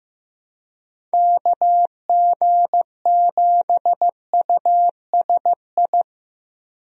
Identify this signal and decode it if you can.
Morse code, heard as a keyed tone, about 15 wpm, 700 Hz. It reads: KG7USI
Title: USA Call Signs - 6 Characters - 15wpm